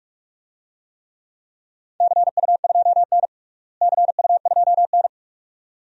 Morse code, heard as a keyed tone, about 46 wpm, 700 Hz. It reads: CU2N CU2N
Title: DXCC-100 Call Signs - 4 Characters - Repeated 2x - 45wpm